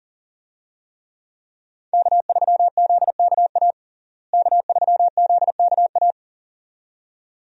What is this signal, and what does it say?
Morse code, heard as a keyed tone, about 40 wpm, 700 Hz. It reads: K3ZKA K3ZKA
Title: USA Call Signs - 5 Characters - Repeated 2x - 40wpm